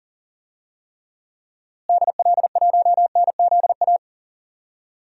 Morse code, heard as a keyed tone, about 40 wpm, 700 Hz. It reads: DL1NZA
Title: DXCC-100 Call Signs - 6 Characters - Repeated 3x - 40wpm